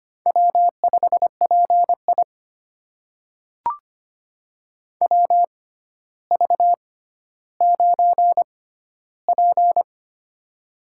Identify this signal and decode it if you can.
Morse code, heard as a keyed tone, about 25 wpm, spaced as for 13 wpm, 700 Hz. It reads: W5PI WV9P
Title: USA Call Signs - 4 Characters - 6x Character Spacing - 25wpm